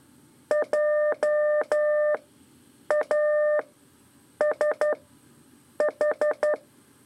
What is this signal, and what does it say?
Morse code, heard as a keyed tone, about 12 wpm, spaced as for 7 wpm, 600 Hz. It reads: JASH
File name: 3 lezione CW 2 AIR